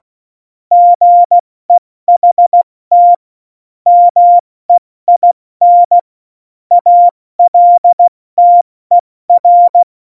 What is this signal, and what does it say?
Morse code, heard as a keyed tone, about 16 wpm, spaced as for 14 wpm, 700 Hz. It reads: GEHT MEIN ALTER
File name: Morsekurs / Witze / Witz 27 Juni 17, Tempo 70 F80